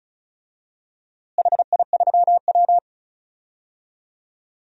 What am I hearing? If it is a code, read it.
Morse code, heard as a keyed tone, about 34 wpm, 700 Hz. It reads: HI3W
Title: DXCC-100 Call Signs - 4 Characters - Repeated 2x - 35wpm